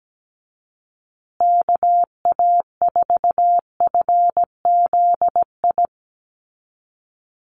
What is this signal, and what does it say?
Morse code, heard as a keyed tone, about 17 wpm, 700 Hz. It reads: KA4FZI
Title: USA Call Signs Repeated 2x 17wpm